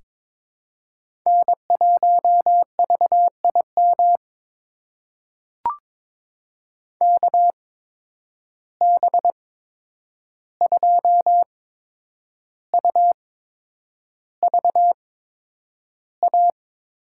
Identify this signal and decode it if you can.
Morse code, heard as a keyed tone, about 22 wpm, spaced as for 9 wpm, 700 Hz. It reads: N1VIM KB2UVA
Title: USA Call Signs - 8x Character Spacing - 22wpm